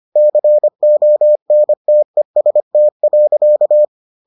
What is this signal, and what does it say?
Morse code, heard as a keyed tone, about 25 wpm, 600 Hz. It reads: CONTEST.